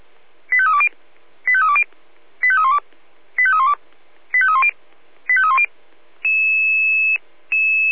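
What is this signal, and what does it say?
Five-tone selective call (ZVEI), heard as quick runs of five tones, tones of about 70 ms. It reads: 85320 85320 85312 85312 85310 85310
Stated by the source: ZVEI five-tone call received by SDR.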